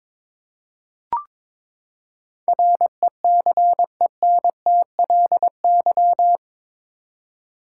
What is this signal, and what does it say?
Morse code, heard as a keyed tone, about 22 wpm, 700 Hz. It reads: RECENTLY
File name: practice / 22wpm / Eight Letter Words - 22wpm